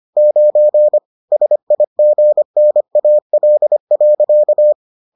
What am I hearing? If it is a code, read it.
Morse code, heard as a keyed tone, about 25 wpm, 600 Hz. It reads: 9 SIGNAL.